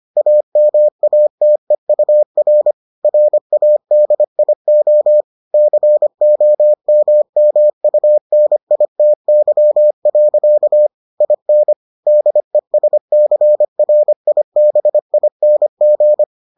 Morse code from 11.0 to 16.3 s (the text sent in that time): IN DESCRIBING